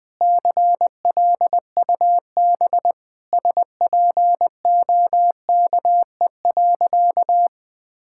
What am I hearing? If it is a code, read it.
Morse code, heard as a keyed tone, about 20 wpm, 700 Hz. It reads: CLUB SPOKE.
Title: Sentences with 4-Word Phrases - 20wpm